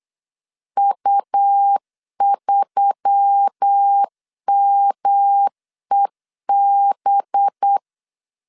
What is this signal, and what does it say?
Morse code, heard as a keyed tone, about 8 wpm, 800 Hz. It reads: U3MEB
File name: CW / DET / 1 6 1500C S B 8wpm 5142